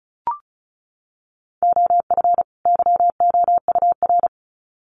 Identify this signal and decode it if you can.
Morse code, heard as a keyed tone, about 34 wpm, 700 Hz. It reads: OF YOUR
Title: Sets of 2 Words 35wpm